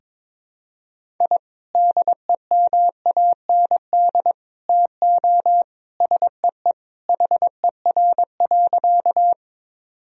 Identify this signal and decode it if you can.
Morse code, heard as a keyed tone, about 22 wpm, 700 Hz. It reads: I DEMAND TO SEE HER.
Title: Sentences from Top 1000 Words - Review Part 4 - 22wpm